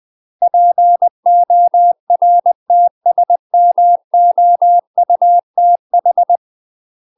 Morse code, heard as a keyed tone, about 20 wpm, 700 Hz. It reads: PORTSMOUTH